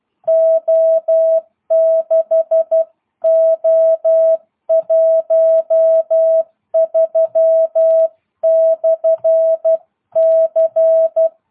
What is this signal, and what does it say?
Morse code, heard as a keyed tone, about 12 wpm, 650 Hz. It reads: O6O13/C
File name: jh8cbh / 2022cw